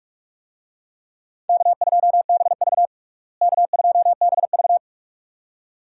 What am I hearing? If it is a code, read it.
Morse code, heard as a keyed tone, about 42 wpm, 700 Hz. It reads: K2BV K2BV
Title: USA Call Signs - 4 Characters - Repeated 2x - 45wpm